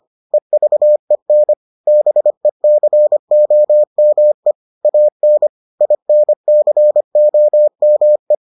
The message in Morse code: EVEN BECOME AN INCOME

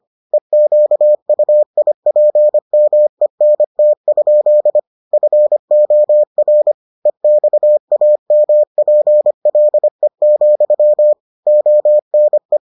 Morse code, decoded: EQUIPMENT? FOR EXAMPLE, ONE

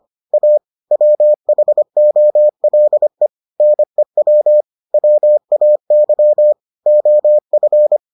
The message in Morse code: A WHOLE NEW WAY OF

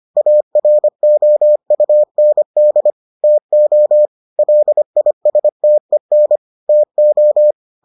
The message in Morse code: AROUND TO LISTEN TO

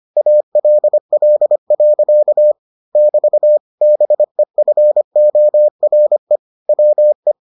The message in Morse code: ALL. <BT> BEFORE WE

